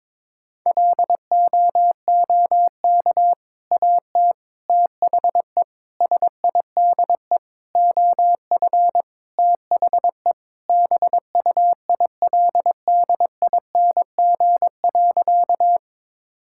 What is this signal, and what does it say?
Morse code, heard as a keyed tone, about 21 wpm, 700 Hz. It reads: LOOK AT THE SIDE OF THE BUILDING.
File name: practice / 22wpm / Sentences with 3-Word Phrases - 22wpm